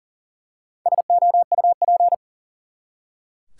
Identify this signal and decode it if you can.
Morse code, heard as a keyed tone, about 40 wpm, 700 Hz. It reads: SOUP
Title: Four Letter Words - 40wpm